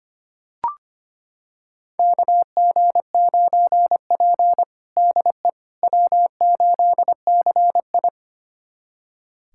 Morse code, heard as a keyed tone, about 25 wpm, 700 Hz. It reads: KG9P DE W8CI
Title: USA Callsign DE Callsign - 4 Characters - 25wpm